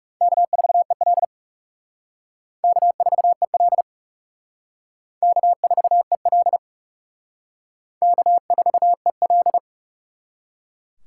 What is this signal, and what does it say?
Morse code, heard as a keyed tone, about 35 wpm, 700 Hz. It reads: K4EL K4EL K4EL K4EL
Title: Speed Racing - USA 1x2 Call Signs 30wpm